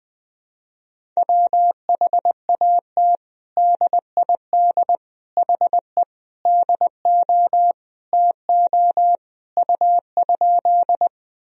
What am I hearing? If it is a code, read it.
Morse code, heard as a keyed tone, about 20 wpm, 700 Hz. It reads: WHAT DID HE DO TO U?
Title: Sentences from Top 100 Words and Abbreviations 20wpm